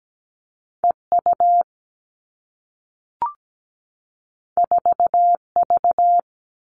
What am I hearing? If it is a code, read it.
Morse code, heard as a keyed tone, about 17 wpm, 700 Hz. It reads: E U 4 V
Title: Two Challenging Letter-Number Combinations 17wpm